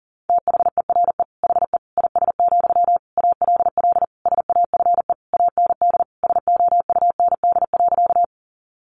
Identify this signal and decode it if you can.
Morse code, heard as a keyed tone, about 40 wpm, 700 Hz. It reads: THERE HE IS, ALL SAFE AND SOUND.
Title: Sentences from Top 900 Words 40wpm